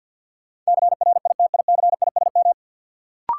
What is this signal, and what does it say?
Morse code, heard as a keyed tone, about 48 wpm, 700 Hz. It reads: CRITICISM
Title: Sentences from Top 1300 Words - New Word Drill - 50wpm